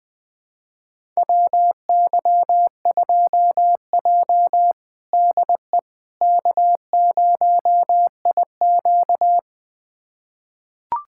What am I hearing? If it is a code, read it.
Morse code, heard as a keyed tone, about 20 wpm, 700 Hz. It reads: WY2J DE K0IQ
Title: USA Callsign DE Callsign - 4 Characters - 20wpm